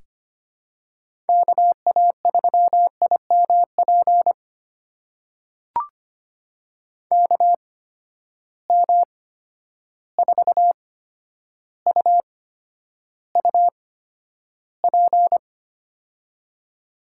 Morse code, heard as a keyed tone, about 25 wpm, spaced as for 11 wpm, 700 Hz. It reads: KA3IMP KM4UUP
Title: USA Call Signs - 6 Characters - 8x Character Spacing - 25wpm